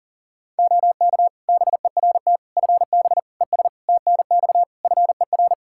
Morse code, heard as a keyed tone, about 40 wpm, 700 Hz. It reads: OK BERT FB ES TNX FER
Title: Practice QSO - Exchange 2 - Info - 40wpm